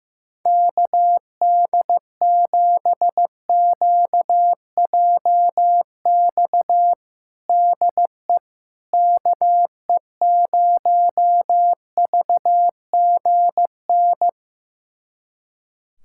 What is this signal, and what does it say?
Morse code, heard as a keyed tone, about 15 wpm, 700 Hz. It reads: KD7QJX DE KE0VGN
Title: USA Callsign DE Callsign 15wpm